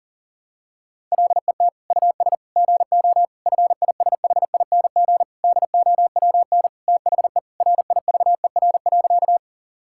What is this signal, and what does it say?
Morse code, heard as a keyed tone, about 40 wpm, 700 Hz. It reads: LET US GO FISHING DOWN THE RIVER.
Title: Sentences from Top 1000 Words - Review Part 5 - 40wpm